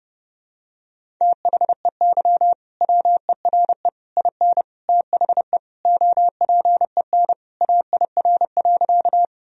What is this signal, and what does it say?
Morse code, heard as a keyed tone, about 30 wpm, 700 Hz. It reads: THEY WERE IN THE OPEN AIR.
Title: Sentences from Top 1000 Words - Review Part 1 - 30wpm